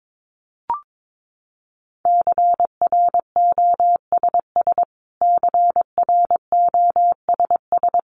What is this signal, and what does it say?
Morse code, heard as a keyed tone, about 22 wpm, 700 Hz. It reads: CROSS CROSS
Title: Top 300 Words Repeated 22wpm